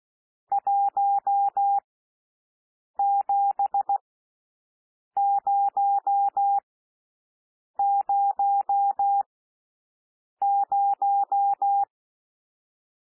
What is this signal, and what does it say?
Morse code, heard as a keyed tone, about 16 wpm, spaced as for 6 wpm, 800 Hz. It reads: 17000
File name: SOUND NUMBERS 16 wpm